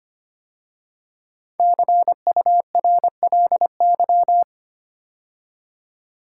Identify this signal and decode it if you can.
Morse code, heard as a keyed tone, about 25 wpm, 700 Hz. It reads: CURLY
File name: 5 Letter Words - First 3 Letters Given - 25wpm